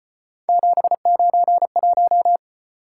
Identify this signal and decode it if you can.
Morse code, heard as a keyed tone, about 33 wpm, 700 Hz. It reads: 791